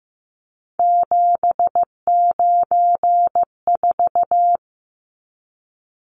Morse code, heard as a keyed tone, about 15 wpm, 700 Hz. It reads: 794
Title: Numbers - 3-Digit 15wpm